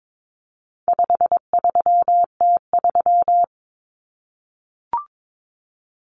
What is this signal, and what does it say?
Morse code, heard as a keyed tone, about 22 wpm, 700 Hz. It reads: 53T3
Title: Common Cut Numbers - 4-Digit 22wpm